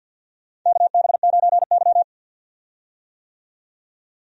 Morse code, heard as a keyed tone, about 46 wpm, 700 Hz. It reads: KB9Y
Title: USA Call Signs - 4 Characters - 50wpm